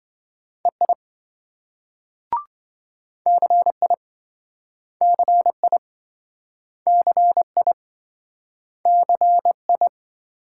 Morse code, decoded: EI CI CI CI CI

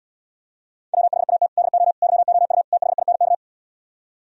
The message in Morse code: WHAT DO YOU THINK

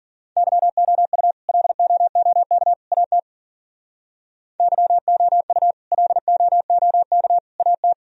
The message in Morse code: YOU LOOK AT YOU LOOK AT

